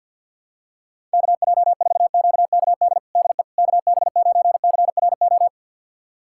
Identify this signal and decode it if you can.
Morse code, heard as a keyed tone, about 47 wpm, 700 Hz. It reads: KJ4QKD DE KB9CRO
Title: USA Callsign DE Callsign - 6 Characters - 50wpm